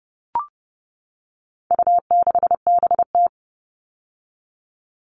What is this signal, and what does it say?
Morse code, heard as a keyed tone, about 30 wpm, 700 Hz. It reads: U6BT
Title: All Cut Numbers - 4-Digit 30wpm